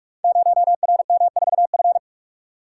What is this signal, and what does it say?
Morse code, heard as a keyed tone, about 44 wpm, 690 Hz. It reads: 0RM4F
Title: Code Groups - 5 Letter:Numbers - 45wpm